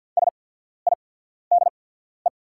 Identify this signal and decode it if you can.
Morse code, heard as a keyed tone, about 48 wpm, spaced as for 14 wpm, 690 Hz. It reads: SIDE
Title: 8x Character Spacing - Top 300 Words - 50wpm